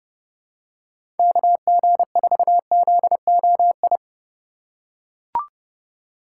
Morse code, heard as a keyed tone, about 30 wpm, 700 Hz. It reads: KG4ZOI
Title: USA Call Signs - 6 Characters - 2x Character Spacing - 30wpm